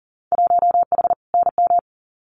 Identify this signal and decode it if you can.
Morse code, heard as a keyed tone, about 40 wpm, 700 Hz. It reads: 1H NM
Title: Winter Field Day Exchanges 40wpm